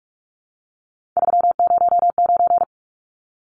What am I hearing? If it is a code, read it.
Morse code, heard as a keyed tone, about 44 wpm, 700 Hz. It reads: 309
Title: Numbers - 3-Digit 45wpm